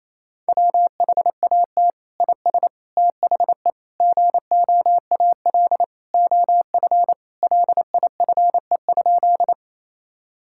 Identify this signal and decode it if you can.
Morse code, heard as a keyed tone, about 28 wpm, 700 Hz. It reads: WHAT IS THE GOAL OF LIFE?
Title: Sentences from Top 1000 Words - Review Part 5 - 28wpm